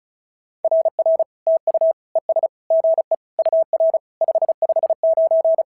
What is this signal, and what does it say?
Morse code, heard as a keyed tone, about 36 wpm, 690 Hz. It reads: RR TU ES GE UR 559
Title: SOTA QSO Practice 35wpm